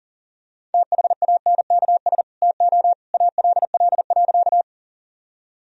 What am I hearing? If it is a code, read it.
Morse code, heard as a keyed tone, about 40 wpm, 700 Hz. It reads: THANKS TO ALL.
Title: Sentences from Top 1000 Words - Review Part 3 - 40wpm